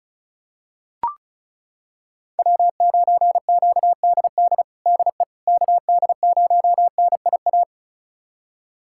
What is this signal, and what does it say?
Morse code, heard as a keyed tone, about 36 wpm, 700 Hz. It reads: W9QDD DE KD0NIA
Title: USA Callsign DE Callsign 35wpm